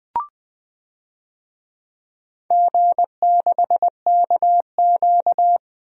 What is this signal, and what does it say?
Morse code, heard as a keyed tone, about 20 wpm, 700 Hz. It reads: G6KQ
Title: DXCC-100 Call Signs - 4 Characters - Repeated 3x - 20wpm